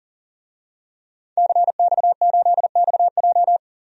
Morse code, heard as a keyed tone, about 37 wpm, 700 Hz. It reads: CX8XJ
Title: DXCC-100 Call Signs - 5 Characters - 40wpm